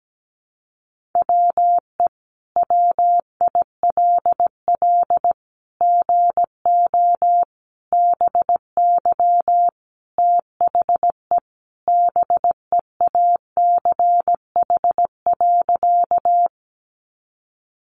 Morse code, heard as a keyed tone, about 17 wpm, 700 Hz. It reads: WE WILL GO BY THE BEACH.
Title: Sentences from Top 1500 Words 17wpm